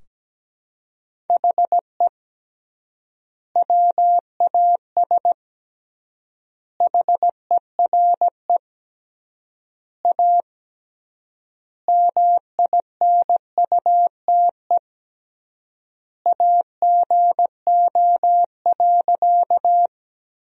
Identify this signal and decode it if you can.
Morse code, heard as a keyed tone, about 17 wpm, 700 Hz. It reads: HE WAS HERE A MINUTE AGO.
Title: 3x Word-Spacing - Sentences from Top 500 Words - 17wpm